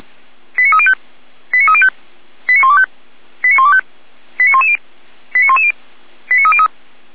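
Five-tone selective call (ZVEI), heard as quick runs of five tones, tones of about 70 ms. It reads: 89305 89305 89125 89125 89110 89110 89303
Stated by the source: Five-tone ZVEI selective call received on an SDR.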